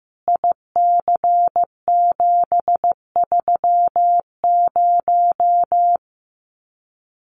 Morse code, heard as a keyed tone, about 15 wpm, 700 Hz. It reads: IC730
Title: Common QSO Phrases 15wpm